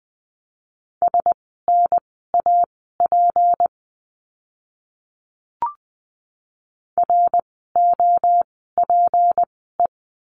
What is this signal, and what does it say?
Morse code, heard as a keyed tone, about 20 wpm, spaced as for 14 wpm, 700 Hz. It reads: SNAP ROPE
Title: 153 Lesson 25f - Letter p - Words 20wpm